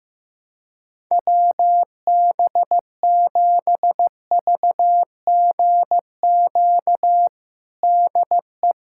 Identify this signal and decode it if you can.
Morse code, heard as a keyed tone, about 15 wpm, 700 Hz. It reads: WB7VGQ DE
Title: USA Callsign DE Callsign - 6 Characters - 15wpm